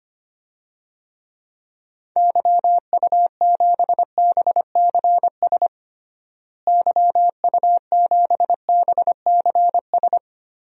Morse code, heard as a keyed tone, about 25 wpm, 700 Hz. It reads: YU7BCS YU7BCS
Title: DXCC-100 Call Signs - 6 Characters - Repeated 2x - 25wpm